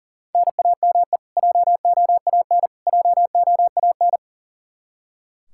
Morse code, heard as a keyed tone, about 40 wpm, 700 Hz. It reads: NAME JOAN JOAN
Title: Practice QSO - Exchange 1 - Beginning - 40wpm